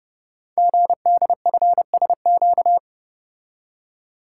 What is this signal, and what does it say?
Morse code, heard as a keyed tone, about 30 wpm, 690 Hz. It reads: GDFSQ
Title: Code Groups - 5 Letters - 30wpm